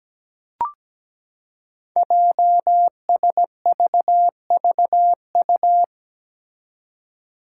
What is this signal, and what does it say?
Morse code, heard as a keyed tone, about 17 wpm, 700 Hz. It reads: JSVVU
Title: Five Challenging Letter-Number Combinations 17wpm